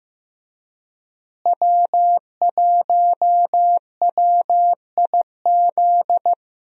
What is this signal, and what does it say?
Morse code, heard as a keyed tone, about 15 wpm, 700 Hz. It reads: W1WIZ
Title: USA Call Signs - 5 Characters - Repeated 3x - 15wpm